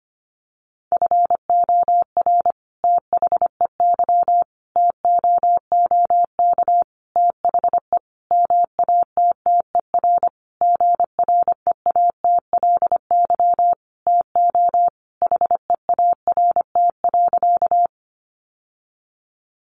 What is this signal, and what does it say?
Morse code, heard as a keyed tone, about 25 wpm, 700 Hz. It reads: FOR THEY TOOK THE MATTER GREATLY TO HEART.
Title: Sentences from Top 1400 Words 25wpm